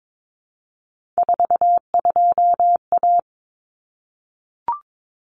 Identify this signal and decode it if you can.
Morse code, heard as a keyed tone, about 22 wpm, 700 Hz. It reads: 42A
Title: Common Cut Numbers - 3-Digit 22wpm